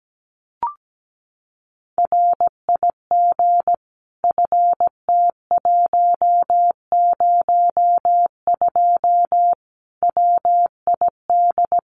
Morse code, heard as a keyed tone, about 17 wpm, 700 Hz. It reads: RIG FT102 WID
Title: Common QSO Phrases 17wpm